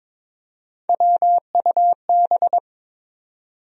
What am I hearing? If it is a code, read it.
Morse code, heard as a keyed tone, about 22 wpm, 700 Hz. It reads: WUB
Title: USA Call Sign Postfixes - 3 Letters 22wpm